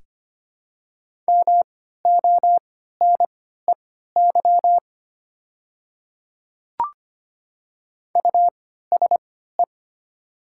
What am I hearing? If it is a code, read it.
Morse code, heard as a keyed tone, about 25 wpm, spaced as for 14 wpm, 700 Hz. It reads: MONEY USE
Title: 3x Character Spacing - Top 500 Words - 25wpm